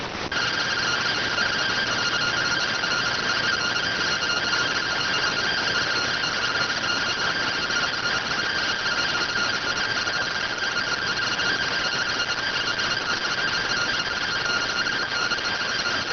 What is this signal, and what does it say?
Two-tone teleprinter signal, about 170 Hz shift.